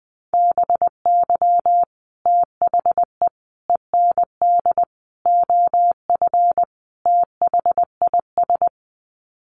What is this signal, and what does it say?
Morse code, heard as a keyed tone, about 20 wpm, 700 Hz. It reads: BY THE END OF THIS
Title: Sets of 5 Words Encore 20wpm